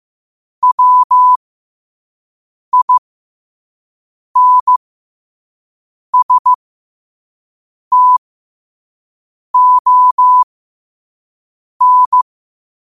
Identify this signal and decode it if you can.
Morse code, heard as a keyed tone, about 15 wpm, spaced as for 5 wpm, 1000 Hz. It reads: WINSTON
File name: qotdfiles / 5wpm